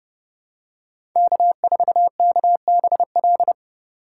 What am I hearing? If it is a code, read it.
Morse code, heard as a keyed tone, about 32 wpm, 700 Hz. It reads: K4KBL
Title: K1USN SST QSO Practice - 30wpm